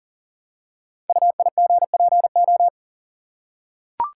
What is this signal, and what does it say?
Morse code, heard as a keyed tone, about 39 wpm, 700 Hz. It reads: UIGPO
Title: Code Groups - 5 Letters - 40wpm